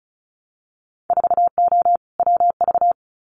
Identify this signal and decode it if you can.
Morse code, heard as a keyed tone, about 34 wpm, 700 Hz. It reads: 4O WV
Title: Winter Field Day Exchanges Encore 35wpm